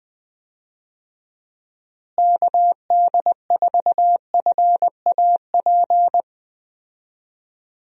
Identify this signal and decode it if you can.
Morse code, heard as a keyed tone, about 20 wpm, 700 Hz. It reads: KD4FAP
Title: USA Call Signs - 6 Characters - Repeated 2x - 20wpm